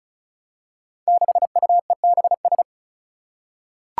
Morse code, heard as a keyed tone, about 36 wpm, 700 Hz. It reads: 6UEBS
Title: Five Challenging Letter-Number Combinations 35wpm